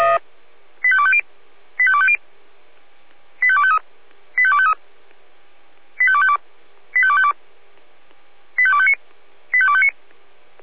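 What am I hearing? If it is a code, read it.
Five-tone selective call (ZVEI), heard as quick runs of five tones, tones of about 70 ms. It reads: 85250 85250 85253 85253 85252 85252 85259 85259
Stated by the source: ZVEI five-tone call 85250 received on SDR.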